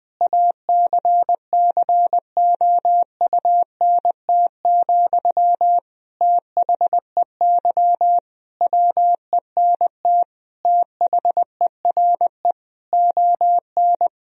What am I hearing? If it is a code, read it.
Morse code, heard as a keyed tone, about 20 wpm, 700 Hz. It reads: ACCOUNT, THEY WENT THERE ON
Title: Sentences from Top 1000 Words - Review Part 4 - 20wpm